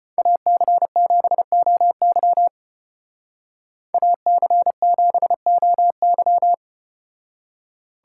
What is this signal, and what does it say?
Morse code, heard as a keyed tone, about 33 wpm, 700 Hz. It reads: AC7OY AC7OY